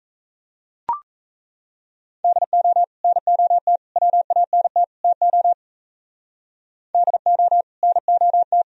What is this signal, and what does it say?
Morse code, heard as a keyed tone, about 34 wpm, 700 Hz. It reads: DO NOT WANT TO DO NOT
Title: Speed-Racing - Sets of 4 Words 28wpm